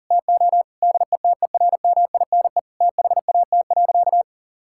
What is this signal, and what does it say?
Morse code, heard as a keyed tone, about 40 wpm, 690 Hz. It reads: TO DETERMINE THAT.
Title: Sentences from Top 1000 Words - Review Part 5 - 40wpm